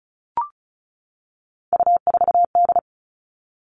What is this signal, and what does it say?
Morse code, heard as a keyed tone, about 34 wpm, 700 Hz. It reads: U4D
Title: All Cut Numbers - 3-Digit 35wpm